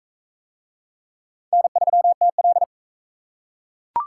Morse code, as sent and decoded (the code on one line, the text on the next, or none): -. ...-- - .-..
N3TL